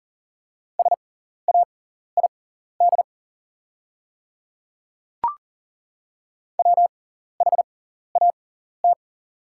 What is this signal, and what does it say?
Morse code, heard as a keyed tone, about 40 wpm, spaced as for 14 wpm, 690 Hz. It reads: SAID WHAT
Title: 6x Character Spacing - Top 100 Words - 40wpm